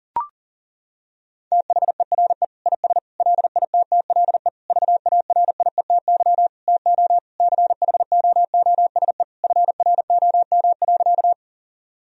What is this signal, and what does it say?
Morse code, heard as a keyed tone, about 40 wpm, 700 Hz. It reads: THERE IS LITTLE VARIETY TO CHOOSE FROM.
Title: Sentences from Top 1000 Words - Review Part 4 - 40wpm